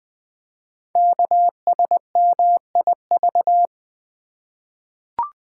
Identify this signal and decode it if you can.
Morse code, heard as a keyed tone, about 21 wpm, 700 Hz. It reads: KSMIV
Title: Code Groups - 5 Letters - 20wpm